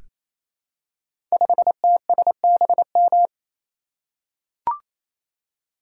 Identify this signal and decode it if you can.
Morse code, heard as a keyed tone, about 28 wpm, 700 Hz. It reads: 5TSBM
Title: Code Groups - 5 Letter:Numbers - 28wpm